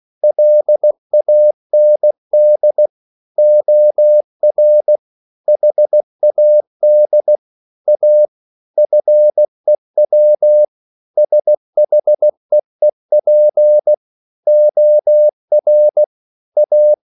Morse code, heard as a keyed tone, about 16 wpm, 600 Hz. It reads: LAND OR HAD A FEW SHEEP OR A